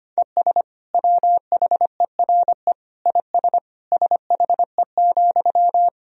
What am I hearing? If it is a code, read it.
Morse code, heard as a keyed tone, about 25 wpm, 700 Hz. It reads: ES WHERE IS SHE,
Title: Sentences from Top 200 Words and Abbreviations 25wpm